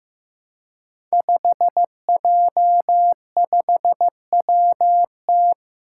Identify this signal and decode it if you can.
Morse code, heard as a keyed tone, about 15 wpm, 700 Hz. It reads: 5J5WT
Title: Five Challenging Letter-Number Combinations 15wpm